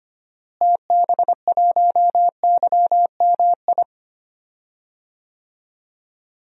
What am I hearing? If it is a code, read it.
Morse code, heard as a keyed tone, about 25 wpm, 700 Hz. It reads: TB1YMI
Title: DXCC-100 Call Signs - 6 Characters - Repeated 2x - 25wpm